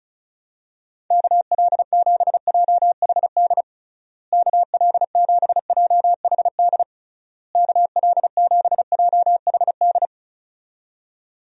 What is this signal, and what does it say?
Morse code, heard as a keyed tone, about 34 wpm, 700 Hz. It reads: KL7JHD KL7JHD KL7JHD
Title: USA Call Signs - 6 Characters - Repeated 3x - 35wpm